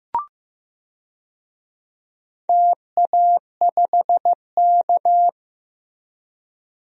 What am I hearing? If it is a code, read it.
Morse code, heard as a keyed tone, about 15 wpm, 700 Hz. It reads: TA5K
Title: DXCC-100 Call Signs - 4 Characters - Repeated 2x - 15wpm